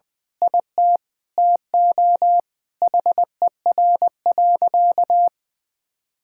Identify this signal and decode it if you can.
Morse code, heard as a keyed tone, about 20 wpm, 700 Hz. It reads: IT TO HER.
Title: Sentences with 2-Word Phrases - 20wpm